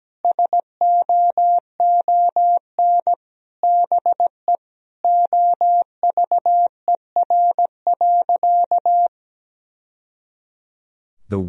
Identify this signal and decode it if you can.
Morse code, heard as a keyed tone, about 17 wpm, 700 Hz. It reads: SOON BE OVER.
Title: Sentences from Top 1000 Words - Review Part 2 - 17wpm